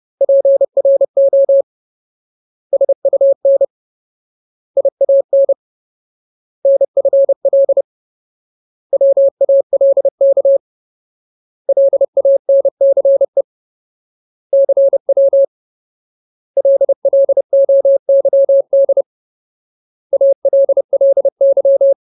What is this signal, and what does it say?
Morse code, heard as a keyed tone, about 30 wpm, 550 Hz. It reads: PRO SUN IAN NFL WALK LANCE CW LLOYD ALLY
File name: Ultimate Word List / 30WPM Ultimate